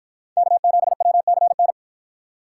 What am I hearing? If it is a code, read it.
Morse code, heard as a keyed tone, about 53 wpm, 700 Hz. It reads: K7WCN